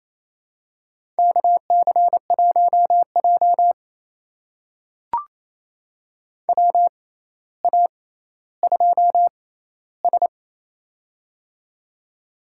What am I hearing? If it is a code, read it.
Morse code, heard as a keyed tone, about 27 wpm, spaced as for 14 wpm, 700 Hz. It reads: KC1J WA2S
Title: USA Call Signs - 4 Characters - 6x Character Spacing - 28wpm